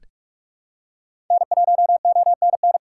Morse code, heard as a keyed tone, about 44 wpm, 700 Hz. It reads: N1ONN
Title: USA Call Signs - 5 Characters - 6x Character Spacing - 45wpm